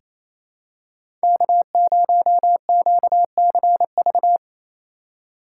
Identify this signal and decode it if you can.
Morse code, heard as a keyed tone, about 28 wpm, 700 Hz. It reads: K0QCV
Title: USA Call Signs - 5 Characters - 2x Character Spacing - 28wpm